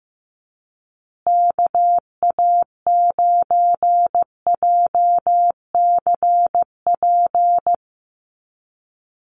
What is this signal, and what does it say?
Morse code, heard as a keyed tone, about 15 wpm, 700 Hz. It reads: KA9JCP
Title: USA Call Signs Repeated 2x 15wpm